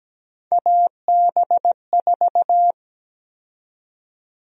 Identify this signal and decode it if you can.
Morse code, heard as a keyed tone, about 17 wpm, 700 Hz. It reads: AB4
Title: USA Call Sign Prefixes - 17wpm